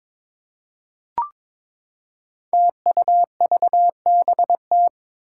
Morse code, heard as a keyed tone, about 22 wpm, 700 Hz. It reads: TUVBT
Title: Five Challenging Letter-Number Combinations 22wpm